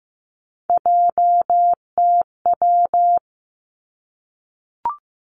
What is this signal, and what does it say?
Morse code, heard as a keyed tone, about 15 wpm, 700 Hz. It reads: JTW
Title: Three Challenging Letter-Number Combinations 15wpm